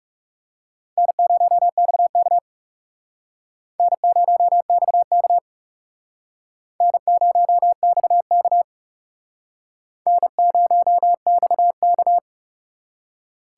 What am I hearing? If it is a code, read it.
Morse code, heard as a keyed tone, about 37 wpm, 700 Hz. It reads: N0XK N0XK N0XK N0XK
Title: Speed Racing - USA 1x2 Call Signs 30wpm